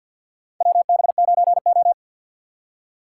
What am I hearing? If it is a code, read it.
Morse code, heard as a keyed tone, about 53 wpm, 700 Hz. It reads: WB9O
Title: USA Call Signs - 4 Characters - 6x Character Spacing - 50wpm